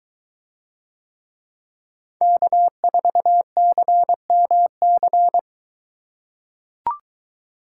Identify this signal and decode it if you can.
Morse code, heard as a keyed tone, about 22 wpm, 700 Hz. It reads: K4CMC